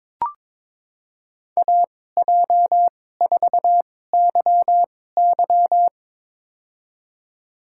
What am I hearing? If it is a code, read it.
Morse code, heard as a keyed tone, about 22 wpm, spaced as for 16 wpm, 700 Hz. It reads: AJ4YY
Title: USA Call Signs - 5 Characters - 2x Character Spacing - 22wpm